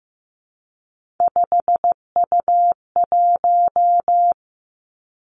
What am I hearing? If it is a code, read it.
Morse code, heard as a keyed tone, about 15 wpm, 700 Hz. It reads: 5U1
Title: Three Challenging Letter-Number Combinations 15wpm